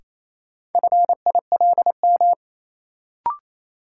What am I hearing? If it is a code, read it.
Morse code, heard as a keyed tone, about 28 wpm, 700 Hz. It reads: FILM
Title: Sentences from Top 1000 Words - Review Part 3 - New Word Drill - 28wpm